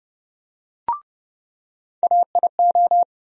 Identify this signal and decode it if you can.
Morse code, heard as a keyed tone, about 30 wpm, 700 Hz. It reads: AIO
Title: USA Call Sign Postfixes - 3 Letters 30wpm